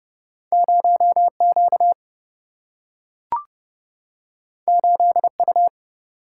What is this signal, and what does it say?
Morse code, heard as a keyed tone, about 30 wpm, 700 Hz. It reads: 0 Q 8 U
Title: Single-Letter Single-Number 30wpm